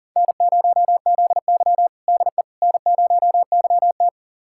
Morse code, heard as a keyed tone, about 37 wpm, 700 Hz. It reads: N0ZY DE N0YT
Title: USA Callsign DE Callsign - 4 Characters - 40wpm